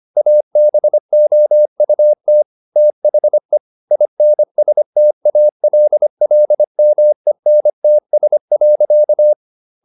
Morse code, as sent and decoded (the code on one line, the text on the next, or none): .- -... --- ..- - / - .... / .. -. ... - .- .-.. .-.. -- . -. - ... .-.-.-
ABOUT THE INSTALLMENTS.